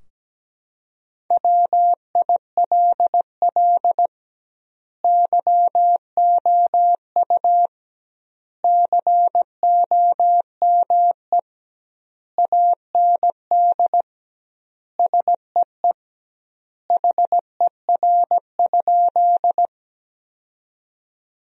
Morse code, heard as a keyed tone, about 17 wpm, 700 Hz. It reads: WILL YOU COME AND SEE HER?
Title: 2x Word-Spacing - Sentences from Top 100 Words - 17wpm